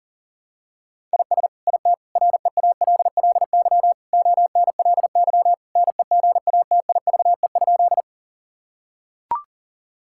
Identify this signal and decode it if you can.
Morse code, heard as a keyed tone, about 40 wpm, 700 Hz. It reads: IS IT REALLY ONLY NEGATIVE?
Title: Sentences from Top 1000 Words - Review Part 6 - 40wpm